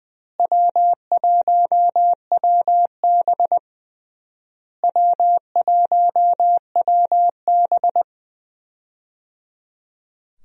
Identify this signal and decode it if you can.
Morse code, heard as a keyed tone, about 20 wpm, 700 Hz. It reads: W1WB W1WB
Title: USA Call Signs - 4 Characters - Repeated 3x - 20wpm